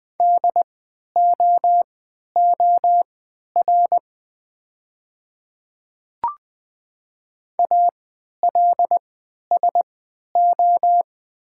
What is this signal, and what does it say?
Morse code, heard as a keyed tone, about 20 wpm, spaced as for 11 wpm, 690 Hz. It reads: DOOR ALSO
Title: 3x Character Spacing - Top 300 Words - 20wpm